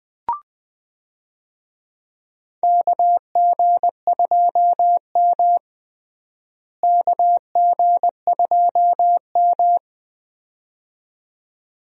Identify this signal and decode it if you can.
Morse code, heard as a keyed tone, about 20 wpm, 700 Hz. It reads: KG2M KG2M
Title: USA Call Signs - 4 Characters - Repeated 2x - 20wpm